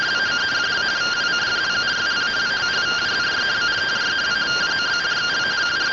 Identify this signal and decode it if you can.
Two-tone teleprinter signal, about 170 Hz shift.